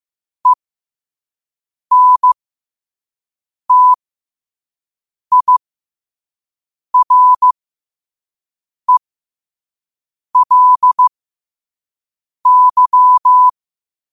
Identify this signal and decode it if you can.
Morse code, heard as a keyed tone, about 15 wpm, spaced as for 5 wpm, 1000 Hz. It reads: ENTIRELY
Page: Quotes for Wed, 13 Aug 2025 in Morse Code at 5 words per minute.